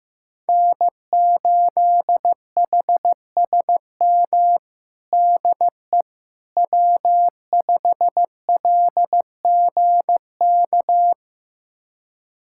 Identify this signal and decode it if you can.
Morse code, heard as a keyed tone, about 15 wpm, 700 Hz. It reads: N8HSM DE W5LGK
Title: USA Callsign DE Callsign - 5 Characters - 15wpm